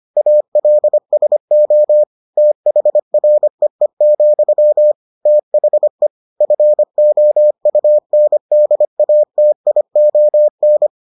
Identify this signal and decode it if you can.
Morse code, heard as a keyed tone, about 25 wpm, 600 Hz. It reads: ALSO THREE, THE FOUNDATION